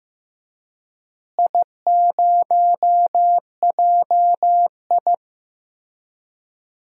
Morse code, heard as a keyed tone, about 15 wpm, 700 Hz. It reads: I0JI
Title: DXCC-100 Call Signs - 4 Characters - Repeated 3x - 15wpm